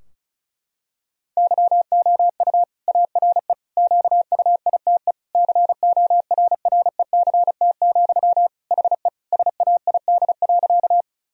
Morse code, heard as a keyed tone, about 34 wpm, 700 Hz. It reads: YOU ARE QUITE CORRECT, HE SAID.
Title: Sentences from Top 1000 Words - Review Part 2 - 35wpm